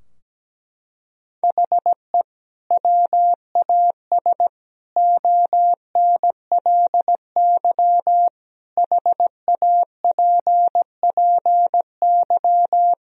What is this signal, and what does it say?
Morse code, heard as a keyed tone, about 17 wpm, 700 Hz. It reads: HE WAS ONLY HAPPY
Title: Sentences from Top 1000 Words - Review Part 4 - 17wpm